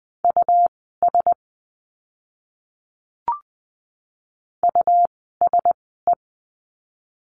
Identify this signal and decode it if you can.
Morse code, heard as a keyed tone, about 20 wpm, spaced as for 14 wpm, 700 Hz. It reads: US USE
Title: 146 Lesson 24f - Letter y - Words 20wpm